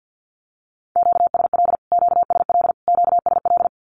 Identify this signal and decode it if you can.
Morse code, heard as a keyed tone, about 50 wpm, 700 Hz. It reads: QSL QSL QSL
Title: Q-Codes Repeated 3x 50wpm